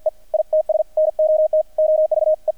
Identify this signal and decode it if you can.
Morse code, heard as a keyed tone, about 50 wpm, 620 Hz, with radio noise. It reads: E ITS NOT OVE
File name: archived / CRSnet